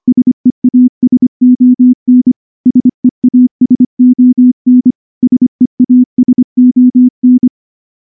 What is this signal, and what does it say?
Morse code, heard as a keyed tone, about 24 wpm, 250 Hz. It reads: SEASON SEASON SEASON
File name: morse-training-25WPM